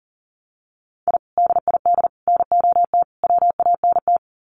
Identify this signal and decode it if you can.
Morse code, heard as a keyed tone, about 39 wpm, 700 Hz. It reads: I DID NOT WANT
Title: Sets of 4 Words 40wpm